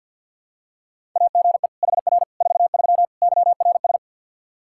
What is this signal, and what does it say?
Morse code, heard as a keyed tone, about 47 wpm, 700 Hz. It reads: AGE HR 43 YRS
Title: Practice QSO - Exchange 3 - More Info - 50wpm